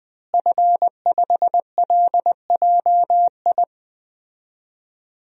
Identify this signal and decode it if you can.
Morse code, heard as a keyed tone, about 20 wpm, 700 Hz. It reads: F5LJI
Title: DXCC-100 Call Signs - 5 Characters - Repeated 3x - 20wpm